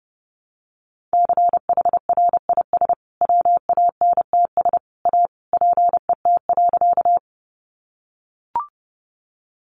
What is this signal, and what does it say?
Morse code, heard as a keyed tone, about 30 wpm, 700 Hz. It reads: CHRIS WANTS A PET.
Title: Additional Easy Phrases 30wpm